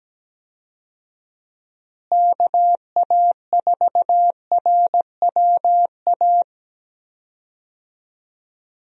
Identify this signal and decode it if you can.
Morse code, heard as a keyed tone, about 17 wpm, 700 Hz. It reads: KA4RWA
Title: USA Call Signs - 6 Characters - Repeated 2x - 17wpm